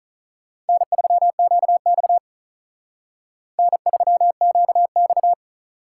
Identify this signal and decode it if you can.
Morse code, heard as a keyed tone, about 43 wpm, 700 Hz. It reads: N3QX N3QX
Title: Speed Racing - USA Call Signs 35wpm